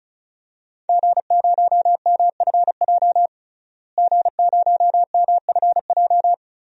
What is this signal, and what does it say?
Morse code, heard as a keyed tone, about 36 wpm, 700 Hz. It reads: G0MFJ G0MFJ
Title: DXCC-100 Call Signs - 5 Characters - Repeated 2x - 35wpm